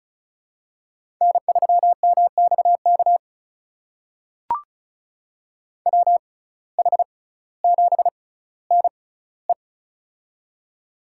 Morse code, heard as a keyed tone, about 36 wpm, spaced as for 18 wpm, 700 Hz. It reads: N3MXK WH7NE
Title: USA Call Signs - 5 Characters - 6x Character Spacing - 35wpm